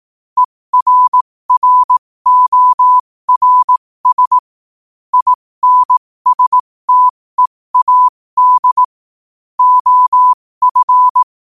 Morse code, heard as a keyed tone, about 18 wpm, spaced as for 15 wpm, 1000 Hz. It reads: ERRORS INSTEAD OF
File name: qotdfiles / 15wpm